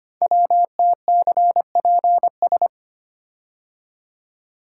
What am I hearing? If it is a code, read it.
Morse code, heard as a keyed tone, about 25 wpm, 700 Hz. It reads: WTCPS
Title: Code Groups - 5 Letters - 25wpm